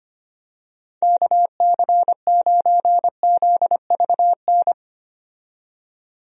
Morse code, heard as a keyed tone, about 25 wpm, 700 Hz. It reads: KC9ZVN
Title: USA Call Signs - 6 Characters - 25wpm